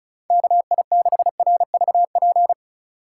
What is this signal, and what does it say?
Morse code, heard as a keyed tone, about 36 wpm, 700 Hz. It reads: KI6RVP
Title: USA Call Signs - 6 Characters - 2x Character Spacing - 35wpm